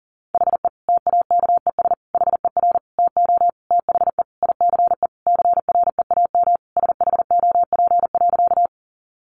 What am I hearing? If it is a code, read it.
Morse code, heard as a keyed tone, about 40 wpm, 700 Hz. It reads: HE TAKES HER TO THE ICE CREAM SHOP.
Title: Easy Phrases 40wpm